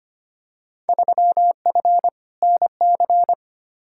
Morse code, heard as a keyed tone, about 25 wpm, 700 Hz. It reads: 3F NC
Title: Field Day Exchanges Encore - 25wpm